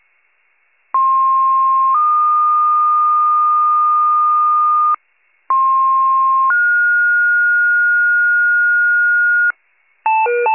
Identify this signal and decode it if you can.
Two-tone page, heard as a one-second tone, then a three-second tone, about 1060 then 1220 Hz, about 1060 then 1430 Hz, with noise. It is heard from 0.9 to 9.5 s.
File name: wcalert